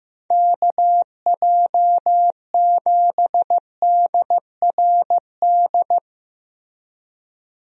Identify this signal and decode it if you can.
Morse code, heard as a keyed tone, about 15 wpm, 700 Hz. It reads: KJ7DRD
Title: USA Call Signs - 6 Characters - Repeated 2x - 15wpm